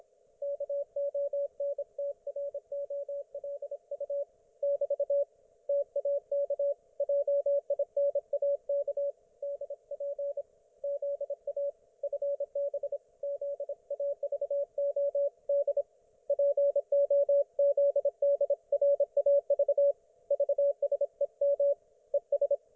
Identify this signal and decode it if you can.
Morse code, heard as a keyed tone, about 27 wpm, 570 Hz, with radio noise. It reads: KONTROLU <BT> TAK JINAK DP ZA FB ZAVOD POZDRAV VSEM ES